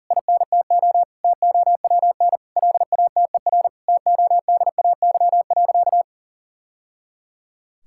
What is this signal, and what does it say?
Morse code, heard as a keyed tone, about 40 wpm, 700 Hz. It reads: INTO TOWN LATER TODAY.
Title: Sentences from Top 1000 Words - Review Part 6 - 40wpm